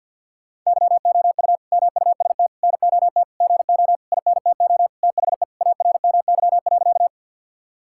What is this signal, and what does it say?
Morse code, heard as a keyed tone, about 50 wpm, 700 Hz. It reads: YOU MUST NOT GO INTO THE ARMY.
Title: Sentences from Top 1000 Words - Review Part 5 - 50wpm